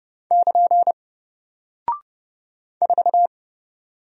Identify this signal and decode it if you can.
Morse code, heard as a keyed tone, about 30 wpm, 700 Hz. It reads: <KN> 4
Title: Single Character (Letter:Number:Punctuation:Prosign) Alt 30wpm